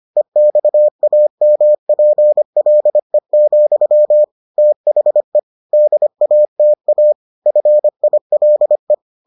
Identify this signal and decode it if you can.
Morse code, heard as a keyed tone, about 25 wpm, 600 Hz. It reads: EXAMPLE, THE DATA FILE